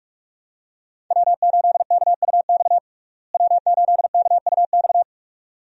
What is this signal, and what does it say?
Morse code, heard as a keyed tone, about 46 wpm, 700 Hz. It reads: W8KUX W8KUX
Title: USA Call Signs - 5 Characters - Repeated 2x - 45wpm